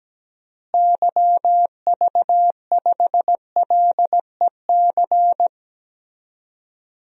Morse code, heard as a keyed tone, about 17 wpm, 700 Hz. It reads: YV5LEC
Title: DXCC-100 Call Signs - 6 Characters - Repeated 3x - 17wpm